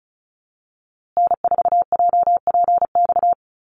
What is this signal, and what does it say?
Morse code, heard as a keyed tone, about 34 wpm, 700 Hz. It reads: N4JPX
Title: USA Call Signs Repeated 3x 35wpm